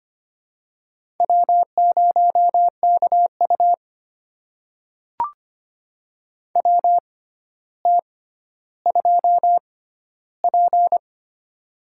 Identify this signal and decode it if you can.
Morse code, heard as a keyed tone, about 25 wpm, spaced as for 13 wpm, 700 Hz. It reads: W0KU WT2P
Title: USA Call Signs - 4 Characters - 6x Character Spacing - 25wpm